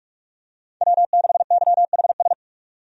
Morse code, heard as a keyed tone, about 46 wpm, 700 Hz. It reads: W6YHS
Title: USA Call Signs - 5 Characters - 45wpm